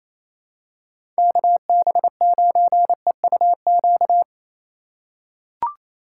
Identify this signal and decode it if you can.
Morse code, heard as a keyed tone, about 28 wpm, 700 Hz. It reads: KB9EUQ
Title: USA Call Signs - 6 Characters - 4x Character Spacing - 28wpm